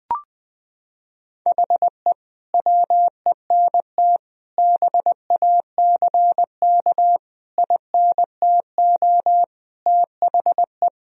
Morse code, heard as a keyed tone, about 19 wpm, 700 Hz. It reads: HE WENT BACK INTO THE
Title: Sentences from Top 1000 Words - Review Part 2 - 20wpm